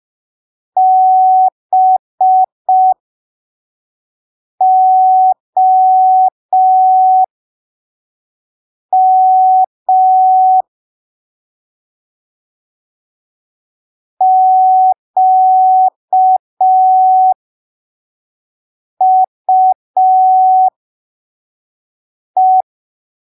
Morse code, heard as a keyed tone, about 5 wpm, spaced as for 3 wpm, 750 Hz. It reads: BOM QUE
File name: arquivos / texto / texto 25 05ppm